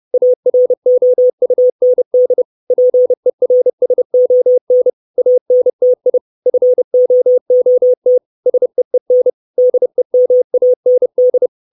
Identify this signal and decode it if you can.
Morse code, heard as a keyed tone, about 30 wpm, 500 Hz. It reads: AROUND PERSON ANTI FOOT SEEN DEMAND
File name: 632 Words 30WPM